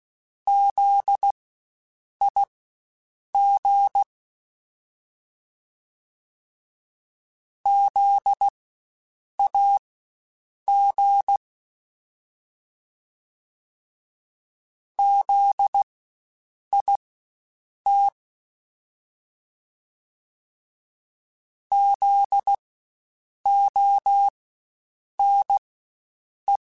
Morse code, heard as a keyed tone, about 16 wpm, spaced as for 7 wpm, 800 Hz. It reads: ZIG ZAG ZIT ZONE
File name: Spaced Alphas / Spaced Alphas mp3s / Zs sp @15